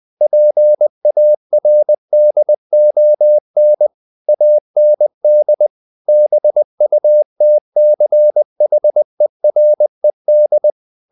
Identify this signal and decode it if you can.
Morse code, heard as a keyed tone, about 20 wpm, 600 Hz. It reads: PARDON AND BUTCHERED
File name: the old man and the sea / the old man and the sea part 1 20wpm